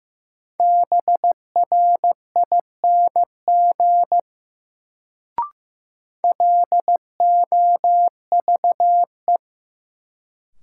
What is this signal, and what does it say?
Morse code, heard as a keyed tone, about 15 wpm, 700 Hz. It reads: BRING LOVE
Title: Sets of 5 Words Encore - New Word Drill - 15wpm